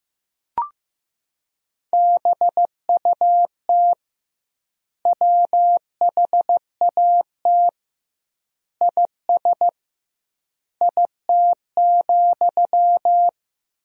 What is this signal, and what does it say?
Morse code, heard as a keyed tone, about 15 wpm, 700 Hz. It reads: BUT WHAT IS IT,
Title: 2x Word-Spacing - Sentences from Top 300 Words - 15wpm